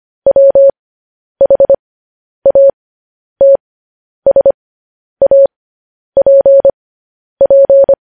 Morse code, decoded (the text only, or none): WHATSAPP